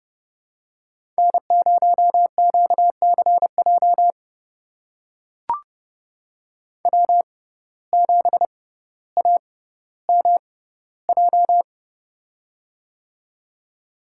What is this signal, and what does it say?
Morse code, heard as a keyed tone, about 32 wpm, spaced as for 16 wpm, 700 Hz. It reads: N0QCJ W7AMJ
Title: USA Call Signs - 5 Characters - 6x Character Spacing - 30wpm